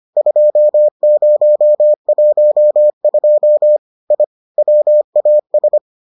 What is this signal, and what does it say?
Morse code, heard as a keyed tone, about 25 wpm, 600 Hz. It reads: 2012 I WAS